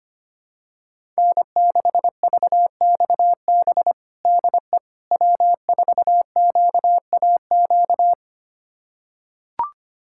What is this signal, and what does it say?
Morse code, heard as a keyed tone, about 25 wpm, 700 Hz. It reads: N6VXB DE W4QAQ
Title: USA Callsign DE Callsign - 5 Characters - 25wpm